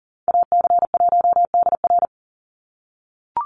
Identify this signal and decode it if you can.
Morse code, heard as a keyed tone, about 40 wpm, 700 Hz. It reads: AC1DR
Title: USA Call Signs 40wpm